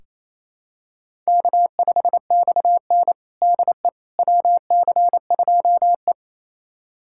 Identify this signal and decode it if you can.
Morse code, heard as a keyed tone, about 28 wpm, 700 Hz. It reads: K5XN DE WC2E
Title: USA Callsign DE Callsign - 4 Characters - 28wpm